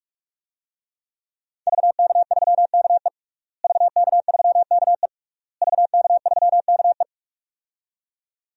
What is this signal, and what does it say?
Morse code, heard as a keyed tone, about 45 wpm, 700 Hz. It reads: VK3KE VK3KE VK3KE
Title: DXCC-100 Call Signs - 5 Characters - Repeated 3x - 45wpm